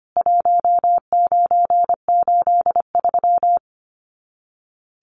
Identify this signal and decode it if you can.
Morse code, heard as a keyed tone, about 25 wpm, 700 Hz. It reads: 1983
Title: Common QSO Phrases 25wpm